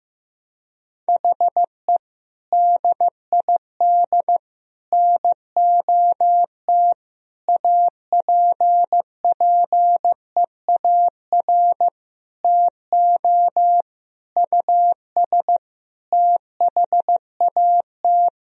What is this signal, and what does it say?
Morse code, heard as a keyed tone, about 15 wpm, 700 Hz. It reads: HE DID NOT APPEAR TO US THAT
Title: Sentences from Top 1000 Words - Review Part 2 - 15wpm